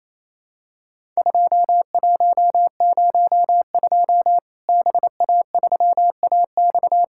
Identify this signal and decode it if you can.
Morse code, heard as a keyed tone, about 28 wpm, 700 Hz. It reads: 2102 BA3AX
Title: Worked All Europe DX Contest - QTC Exchange - 28wpm